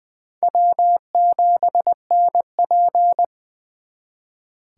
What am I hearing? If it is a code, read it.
Morse code, heard as a keyed tone, about 24 wpm, 700 Hz. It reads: W7NP
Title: Speed Racing - USA 1x2 Call Signs 20wpm